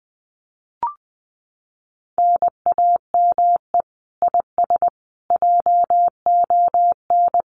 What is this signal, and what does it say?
Morse code, heard as a keyed tone, about 20 wpm, 700 Hz. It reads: NAME IS JON
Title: Common QSO Phrases Encore 20wpm